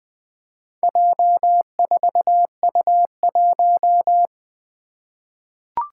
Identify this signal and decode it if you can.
Morse code, heard as a keyed tone, about 20 wpm, 700 Hz. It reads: J4U1
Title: Four Challenging Letter-Number Combinations 20wpm